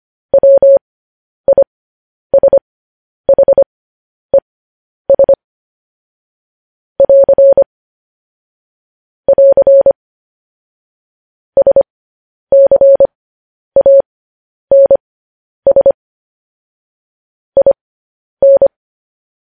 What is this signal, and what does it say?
Morse code, heard as a keyed tone, about 25 wpm, spaced as for 10 wpm, 550 Hz. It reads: WISHES <AR> <AR> SCANS IN
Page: news-headlines-in-morse-code-at-10-wpm